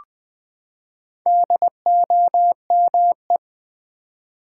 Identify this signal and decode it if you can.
Morse code, heard as a keyed tone, about 20 wpm, 700 Hz. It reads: DOME
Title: Build A Word - 4 Letters At A Time - 8-Letter Words - 20wpm